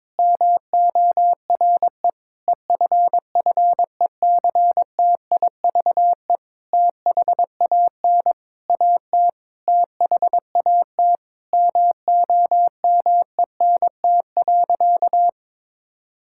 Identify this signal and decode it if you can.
Morse code, heard as a keyed tone, about 21 wpm, 700 Hz. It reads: MORE EFFECTIVE THAN AT THAT MOMENT.
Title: Sentences from Top 1000 Words - Review Part 6 - 22wpm